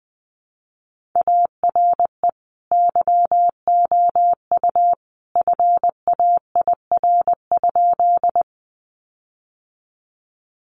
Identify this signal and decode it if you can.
Morse code, heard as a keyed tone, about 20 wpm, 700 Hz. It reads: ARE YOU FAIR?
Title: Sentences from Top 1300 Words 20wpm